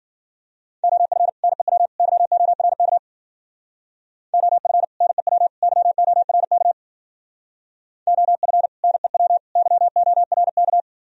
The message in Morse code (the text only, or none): OF NEW YORK OF NEW YORK OF NEW YORK